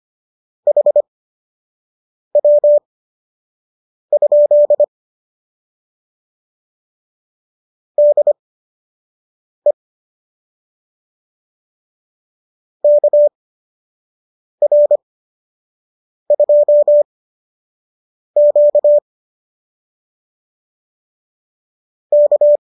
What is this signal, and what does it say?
Morse code, heard as a keyed tone, about 25 wpm, spaced as for 6 wpm, 590 Hz. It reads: HW? DE KR2Q K